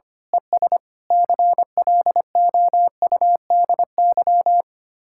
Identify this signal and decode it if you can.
Morse code, heard as a keyed tone, about 24 wpm, 700 Hz. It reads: ES CLOUDY
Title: Speed-Racing Common QSO Phrases Encore - 25wpm